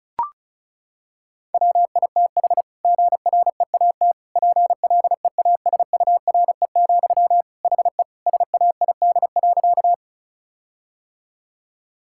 Morse code, heard as a keyed tone, about 34 wpm, 700 Hz. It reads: WITH GREAT PLEASURE, HE SAID.
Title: Sentences from Top 1600 Words 35wpm